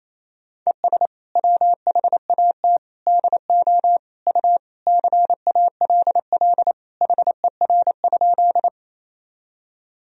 Morse code, heard as a keyed tone, about 28 wpm, 700 Hz. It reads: ES WHAT DO U CALL HER?
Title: Sentences from Top 100 Words and Abbreviations 28wpm